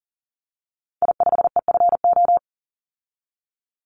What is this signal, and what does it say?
Morse code, heard as a keyed tone, about 41 wpm, 700 Hz. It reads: I5EFO
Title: Call Signs - 40wpm